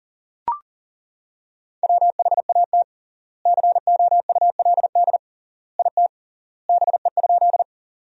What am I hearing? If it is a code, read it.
Morse code, heard as a keyed tone, about 40 wpm, 700 Hz. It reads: WHAT COULD IT BE?
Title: 3x Word-Spacing - Sentences from Top 100 Words - 40wpm